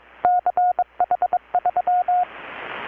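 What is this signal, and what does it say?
Morse code, heard as a keyed tone, about 22 wpm, 720 Hz. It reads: CH3
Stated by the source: Morse from KPH's Night of Nights XX maritime radiotelegraphy event, received in Chiapas, Mexico about 3,800 km away on a Kenwood R-600 with a grounded 10 m long-wire antenna.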